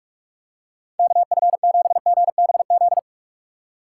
Morse code, heard as a keyed tone, about 46 wpm, 700 Hz. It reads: KF7GBZ
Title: USA Call Signs - 6 Characters - 3x Character Spacing - 45wpm